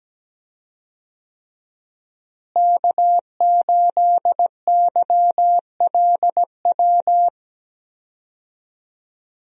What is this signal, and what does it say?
Morse code, heard as a keyed tone, about 17 wpm, 700 Hz. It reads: K8YLW
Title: USA Call Signs - 5 Characters - Repeated 3x - 17wpm